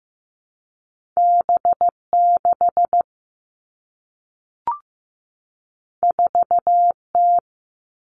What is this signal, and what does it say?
Morse code, heard as a keyed tone, about 15 wpm, 700 Hz. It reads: B6 4T
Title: Two Challenging Letter-Number Combinations 15wpm